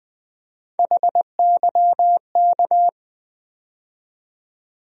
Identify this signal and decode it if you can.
Morse code, heard as a keyed tone, about 20 wpm, 700 Hz. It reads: HYK
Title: USA Call Sign Postfixes - 3 Letters 20wpm